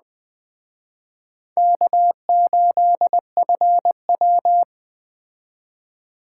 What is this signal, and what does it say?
Morse code, heard as a keyed tone, about 20 wpm, 700 Hz. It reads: K8FW
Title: Speed Racing - USA 1x2 Call Signs 17wpm